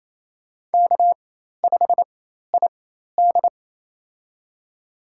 Morse code, heard as a keyed tone, about 28 wpm, spaced as for 13 wpm, 700 Hz. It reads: K5ID
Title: USA Call Signs - 4 Characters - 4x Character Spacing - 28wpm